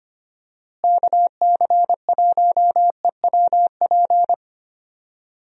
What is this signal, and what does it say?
Morse code, heard as a keyed tone, about 25 wpm, 700 Hz. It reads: KC1EWP
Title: USA Call Signs - 6 Characters - Repeated 2x - 25wpm